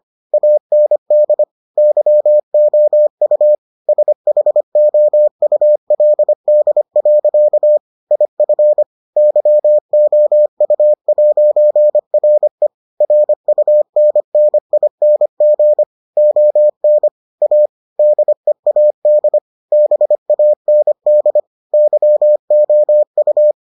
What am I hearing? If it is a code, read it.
Morse code, heard as a keyed tone, about 25 wpm, 600 Hz. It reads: AND YOU SHOULD. IF YOU'RE RUNNING ON A DEAD BAND YOU